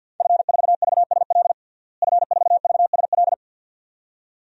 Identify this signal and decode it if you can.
Morse code, heard as a keyed tone, about 52 wpm, 700 Hz. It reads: F4VSL F4VSL
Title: DXCC-100 Call Signs - 5 Characters - Repeated 2x - 50wpm